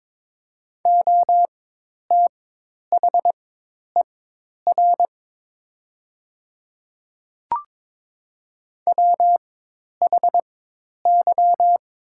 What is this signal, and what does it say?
Morse code, heard as a keyed tone, about 22 wpm, spaced as for 10 wpm, 700 Hz. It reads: OTHER WHY
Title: 4x Character Spacing - Top 200 Words - 22wpm